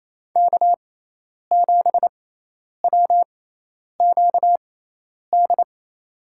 Morse code, heard as a keyed tone, about 28 wpm, spaced as for 10 wpm, 700 Hz. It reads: K7WQD
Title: USA Call Signs - 6x Character Spacing - 28wpm